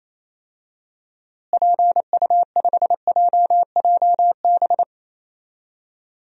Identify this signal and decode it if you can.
Morse code, heard as a keyed tone, about 28 wpm, 700 Hz. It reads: PU5JJB
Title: DXCC-100 Call Signs - 6 Characters - Repeated 3x - 28wpm